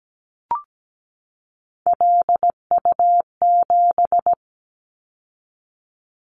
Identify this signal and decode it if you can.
Morse code, heard as a keyed tone, about 17 wpm, 700 Hz. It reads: LU7
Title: Callsign Prefixes for Top 50 DXCC - 17wpm